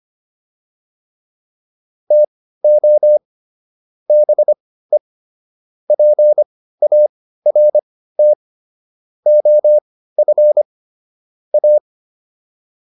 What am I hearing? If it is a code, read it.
Morse code, heard as a keyed tone, about 25 wpm, spaced as for 15 wpm, 600 Hz. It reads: TO BE PART OF A
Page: intermediate-practice-files